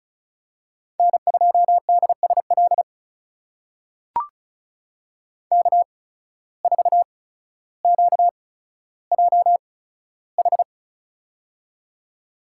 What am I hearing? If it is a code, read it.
Morse code, heard as a keyed tone, about 36 wpm, spaced as for 15 wpm, 700 Hz. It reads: N2DSL K4QJH
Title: USA Call Signs - 5 Characters - 8x Character Spacing - 35wpm